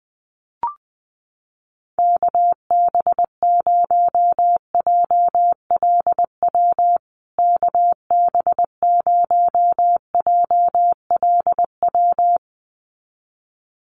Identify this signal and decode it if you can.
Morse code, heard as a keyed tone, about 20 wpm, 700 Hz. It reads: KB0JLW KB0JLW
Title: USA Call Signs Repeated 2x 20wpm